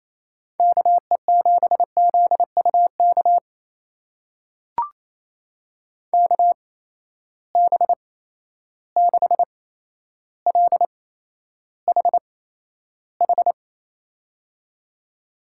Morse code, heard as a keyed tone, about 28 wpm, spaced as for 12 wpm, 700 Hz. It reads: KE7ZUK KB6LHH
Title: USA Call Signs - 6 Characters - 8x Character Spacing - 28wpm